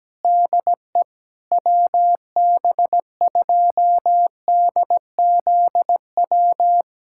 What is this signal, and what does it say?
Morse code, heard as a keyed tone, about 17 wpm, 700 Hz. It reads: DE WB2DZW
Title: USA Callsign DE Callsign - 6 Characters - 17wpm